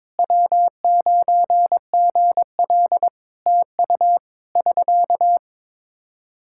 Morse code, heard as a keyed tone, about 22 wpm, 700 Hz. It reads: W9GL TU <SK>
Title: K1USN SST QSO Practice 22wpm